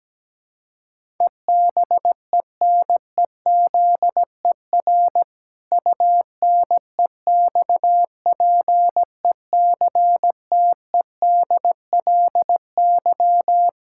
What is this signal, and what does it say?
Morse code, heard as a keyed tone, about 17 wpm, 700 Hz. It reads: EBENEZER UNEXPECTEDLY
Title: Pangram Challenge v2 - 17wpm